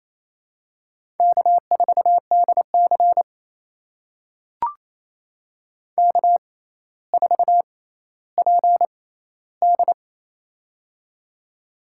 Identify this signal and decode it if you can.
Morse code, heard as a keyed tone, about 28 wpm, spaced as for 14 wpm, 700 Hz. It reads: K4DC K4PD
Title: USA Call Signs - 4 Characters - 6x Character Spacing - 28wpm